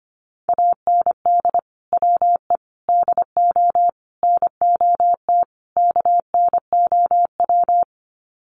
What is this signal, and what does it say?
Morse code, heard as a keyed tone, about 25 wpm, 700 Hz. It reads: AND WE DO NOT KNOW
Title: Sets of 5 Words Encore 25wpm